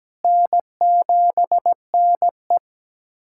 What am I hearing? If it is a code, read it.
Morse code, heard as a keyed tone, about 17 wpm, 700 Hz. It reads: N7NE
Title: USA Callsign DE Callsign - 4 Characters - 17wpm